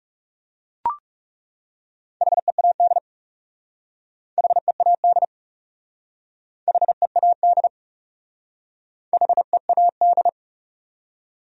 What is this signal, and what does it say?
Morse code, heard as a keyed tone, about 39 wpm, 700 Hz. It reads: HEAD HEAD HEAD HEAD